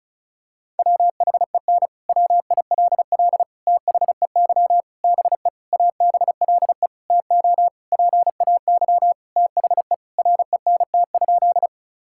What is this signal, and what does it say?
Morse code, heard as a keyed tone, about 36 wpm, 700 Hz. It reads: WHEN WILL THEY BE ABLE TO PAY THE RENT?
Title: Sentences from Top 1000 Words - Review Part 5 - 35wpm